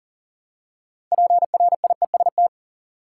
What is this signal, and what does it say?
Morse code, heard as a keyed tone, about 40 wpm, 700 Hz. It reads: PRIEST
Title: Six Letter Words - 40wpm